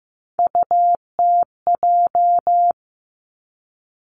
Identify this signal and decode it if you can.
Morse code, heard as a keyed tone, about 15 wpm, 700 Hz. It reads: UTJ
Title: Three Challenging Letter-Number Combinations 15wpm